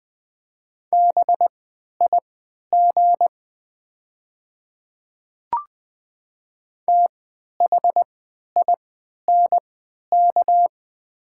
Morse code, heard as a keyed tone, about 20 wpm, spaced as for 11 wpm, 700 Hz. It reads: BIG THINK
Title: 3x Character Spacing - Top 200 Words - 20wpm